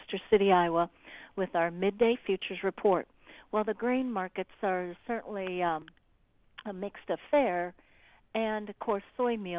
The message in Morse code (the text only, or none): none